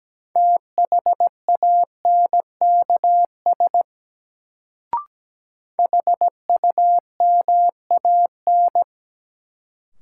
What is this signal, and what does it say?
Morse code, heard as a keyed tone, about 17 wpm, 700 Hz. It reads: THANKS HUMAN
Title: Sets of 5 Words - New Word Drill - 17wpm